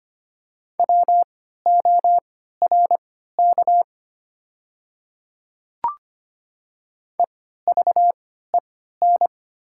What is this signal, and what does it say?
Morse code, heard as a keyed tone, about 25 wpm, spaced as for 14 wpm, 700 Hz. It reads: WORK EVEN
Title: 3x Character Spacing - Top 200 Words - 25wpm